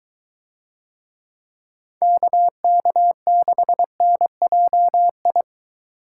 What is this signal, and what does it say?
Morse code, heard as a keyed tone, about 23 wpm, 700 Hz. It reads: KK6NJI